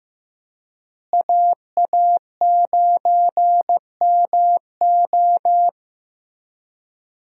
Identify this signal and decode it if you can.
Morse code, heard as a keyed tone, about 15 wpm, 700 Hz. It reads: AA9MO
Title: USA Call Signs - 5 Characters - Repeated 2x - 15wpm